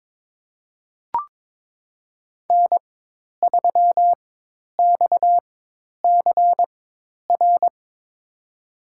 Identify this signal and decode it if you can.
Morse code, heard as a keyed tone, about 22 wpm, spaced as for 10 wpm, 700 Hz. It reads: N3XCR
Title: USA Call Signs - 5 Characters - 4x Character Spacing - 22wpm